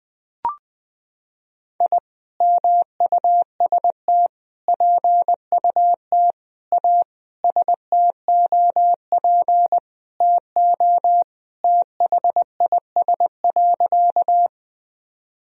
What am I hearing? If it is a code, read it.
Morse code, heard as a keyed tone, about 21 wpm, 700 Hz. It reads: I MUST PUT A STOP TO THIS.
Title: Sentences from Top 1000 Words - Review Part 1 - 20wpm